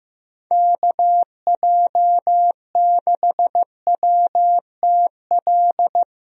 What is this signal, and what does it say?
Morse code, heard as a keyed tone, about 15 wpm, 700 Hz. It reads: KJ6WTL
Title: USA Call Signs - 6 Characters - Repeated 3x - 15wpm